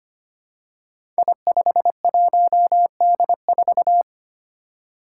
Morse code, heard as a keyed tone, about 25 wpm, 700 Hz. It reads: I51D4
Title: Code Groups - 5 Letter:Numbers - 25wpm